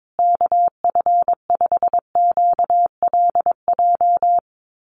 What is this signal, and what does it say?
Morse code, heard as a keyed tone, about 22 wpm, 700 Hz. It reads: KF5QLJ
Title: USA Call Signs Repeated 3x 22wpm